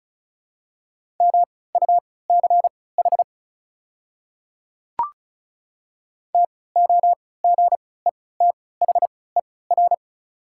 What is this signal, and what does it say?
Morse code, heard as a keyed tone, about 36 wpm, spaced as for 20 wpm, 690 Hz. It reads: MUCH TOGETHER
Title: 3x Character Spacing - Top 500 Words - 35wpm